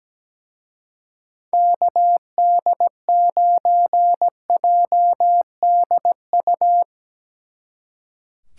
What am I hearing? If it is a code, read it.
Morse code, heard as a keyed tone, about 17 wpm, 700 Hz. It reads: KD9JDU
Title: Speed Racing - USA 2x3 Call Signs 17wpm